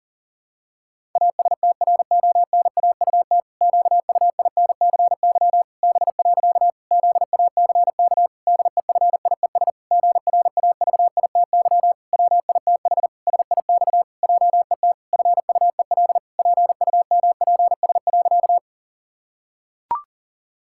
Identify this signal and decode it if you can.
Morse code, heard as a keyed tone, about 40 wpm, 700 Hz. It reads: ASTRONAUT QUINCY B. ZACK DEFIES GRAVITY WITH SIX JET FUEL PUMPS.
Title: Pangram Challenge v2 - 40wpm